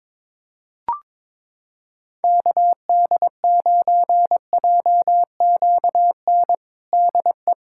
Morse code, heard as a keyed tone, about 22 wpm, 700 Hz. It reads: KD9JQN DE
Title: Practice QSO - Exchange 4 - Ending - 22wpm